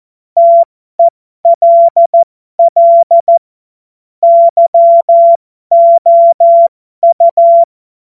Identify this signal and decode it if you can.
Morse code, heard as a keyed tone, about 14 wpm, spaced as for 12 wpm, 680 Hz. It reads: TELL YOU